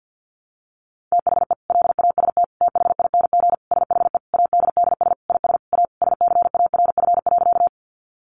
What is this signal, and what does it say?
Morse code, heard as a keyed tone, about 50 wpm, 700 Hz. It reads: THE LAST THING SHE ADDS IS A SCARF.
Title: Easy Phrases 50wpm